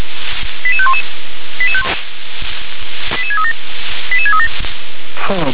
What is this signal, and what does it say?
Five-tone selective call (ZVEI), heard as quick runs of five tones, tones of about 80 ms, with speech. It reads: 88511 88637 88637